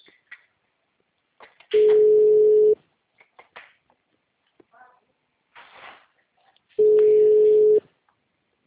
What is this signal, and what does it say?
Ringback tone (Europe, 425 Hz).
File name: out- -2034- - - .4217